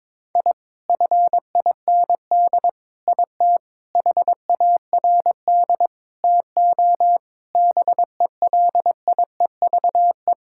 Morse code, heard as a keyed tone, about 22 wpm, 700 Hz. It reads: I FIND IT HARD TO BELIEVE
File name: practice / 22wpm / Sets of 7 Words - 22wpm